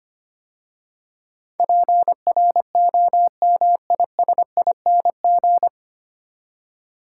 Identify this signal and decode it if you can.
Morse code, heard as a keyed tone, about 25 wpm, 700 Hz. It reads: PROMISING
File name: practice / 25wpm / 9 Letter Words - First 3 Letters Given - 25wpm